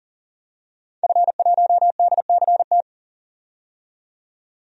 Morse code, heard as a keyed tone, about 40 wpm, 700 Hz. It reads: F1DCT
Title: DXCC-100 Call Signs - 5 Characters - 40wpm